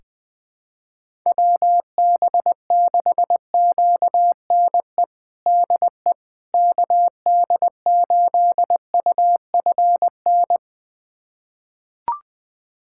Morse code, heard as a keyed tone, about 20 wpm, 700 Hz. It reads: WB6QNE DE KD8UFN
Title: USA Callsign DE Callsign - 6 Characters - 20wpm